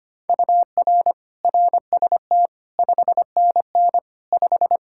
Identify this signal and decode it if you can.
Morse code, heard as a keyed tone, about 25 wpm, 700 Hz. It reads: UR RST 5NN 5
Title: Practice QSO - Exchange 1 - Beginning - 25wpm